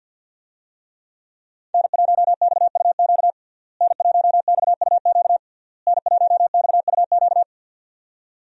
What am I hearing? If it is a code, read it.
Morse code, heard as a keyed tone, about 51 wpm, 700 Hz. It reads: N1XUQ N1XUQ N1XUQ
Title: USA Call Signs - 5 Characters - Repeated 3x - 50wpm